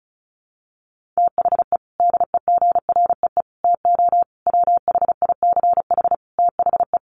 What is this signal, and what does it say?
Morse code, heard as a keyed tone, about 34 wpm, 700 Hz. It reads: THE DEGREE TO WHICH THE
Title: Sets of 5 Words Encore2 35wpm